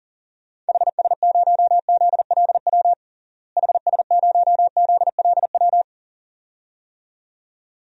Morse code, heard as a keyed tone, about 40 wpm, 700 Hz. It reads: HS0ZLW HS0ZLW
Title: DXCC-100 Call Signs - 6 Characters - Repeated 2x - 40wpm